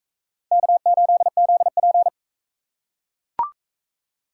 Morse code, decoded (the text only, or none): K8ZP